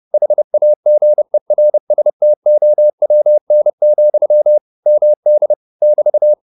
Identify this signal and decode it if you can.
Morse code, heard as a keyed tone, about 30 wpm, 600 Hz. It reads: HAGERSTOWN, MD <BT>